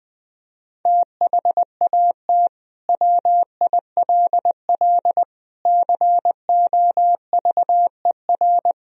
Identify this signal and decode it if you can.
Morse code, heard as a keyed tone, about 20 wpm, 700 Hz. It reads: THAT WILL COVER
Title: Sentences from Top 1000 Words - Review Part 1 - 20wpm